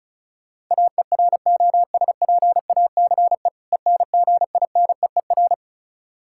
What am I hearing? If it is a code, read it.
Morse code, heard as a keyed tone, about 36 wpm, 700 Hz. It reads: AEROSPACE ENGINEER
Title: K1USN SST QSO Practice 35wpm